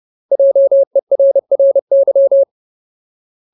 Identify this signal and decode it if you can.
Morse code, heard as a keyed tone, about 30 wpm, 550 Hz. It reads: JERRY
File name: Ultimate Word List / 30WPM Ultimate